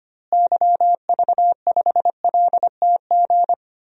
Morse code, heard as a keyed tone, about 25 wpm, 700 Hz. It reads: YV5LTG
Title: DXCC-100 Call Signs - 6 Characters - Repeated 3x - 25wpm